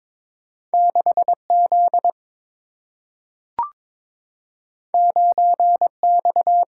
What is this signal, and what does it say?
Morse code, heard as a keyed tone, about 22 wpm, 700 Hz. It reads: 6Z 9X
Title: Single-Letter Single-Number 22wpm